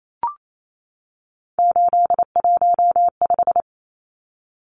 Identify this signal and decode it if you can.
Morse code, heard as a keyed tone, about 28 wpm, 700 Hz. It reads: 815
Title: Numbers - 3-Digit 28wpm